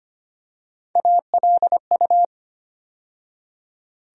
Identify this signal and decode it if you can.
Morse code, heard as a keyed tone, about 25 wpm, 700 Hz. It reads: ALU
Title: USA Call Sign Postfixes - 3 Letters 25wpm